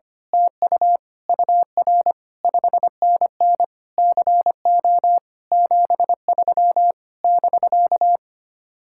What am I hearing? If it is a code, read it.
Morse code, heard as a keyed tone, about 25 wpm, 700 Hz. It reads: TU UR 5NN CO 73 <BK>
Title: Speed-Racing Common QSO Phrases - 25wpm